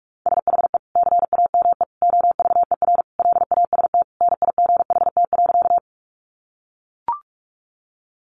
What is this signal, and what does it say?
Morse code, heard as a keyed tone, about 44 wpm, 700 Hz. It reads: SHE CAME OVER LAST NIGHT.
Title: Sentences from Top 300 Words 45wpm